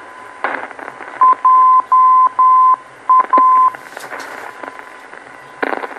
Morse code, heard as a keyed tone, about 10 wpm, 1030 Hz, with radio noise. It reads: JA
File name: beacon program / tracks / 01 JA Beacon